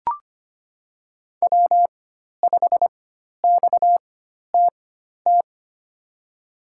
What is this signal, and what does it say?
Morse code, heard as a keyed tone, about 25 wpm, spaced as for 12 wpm, 700 Hz. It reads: W5XTT
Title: USA Call Signs - 5 Characters - 4x Character Spacing - 25wpm